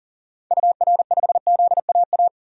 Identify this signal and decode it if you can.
Morse code, heard as a keyed tone, about 40 wpm, 700 Hz. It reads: UR5ZAA
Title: DXCC-100 Call Signs - 6 Characters - 40wpm